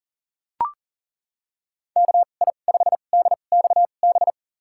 Morse code, heard as a keyed tone, about 40 wpm, spaced as for 29 wpm, 700 Hz. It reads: KI5DXB
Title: USA Call Signs - 2x Character Spacing - 40wpm